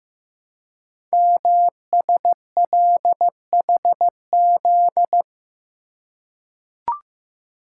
Morse code, heard as a keyed tone, about 15 wpm, 700 Hz. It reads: MSLHZ
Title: Code Groups - 5 Letter:Numbers - 15wpm